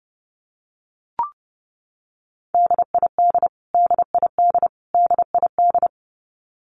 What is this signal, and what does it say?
Morse code, heard as a keyed tone, about 30 wpm, 700 Hz. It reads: DID DID DID
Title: Top 500 Words Repeated 30wpm